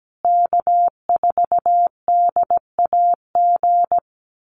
Morse code, heard as a keyed tone, about 17 wpm, 700 Hz. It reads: K4DAG
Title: USA Call Signs Repeated 3x 17wpm